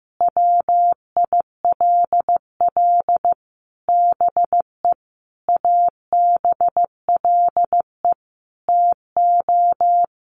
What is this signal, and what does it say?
Morse code, heard as a keyed tone, about 15 wpm, 700 Hz. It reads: WILL BE ABLE TO
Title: Sets of 5 Words Encore 15wpm